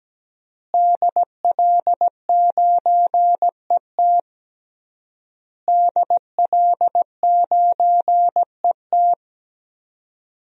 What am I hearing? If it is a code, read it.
Morse code, heard as a keyed tone, about 17 wpm, 700 Hz. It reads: DL9ET DL9ET
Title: DXCC-100 Call Signs - 5 Characters - Repeated 3x - 17wpm